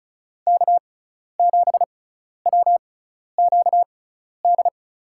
Morse code, heard as a keyed tone, about 34 wpm, spaced as for 12 wpm, 700 Hz. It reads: K7WQD
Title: USA Call Signs - 6x Character Spacing - 35wpm